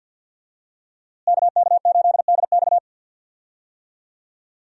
Morse code, heard as a keyed tone, about 53 wpm, 700 Hz. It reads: KK8DX
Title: USA Call Signs - 5 Characters - Repeated 2x - 50wpm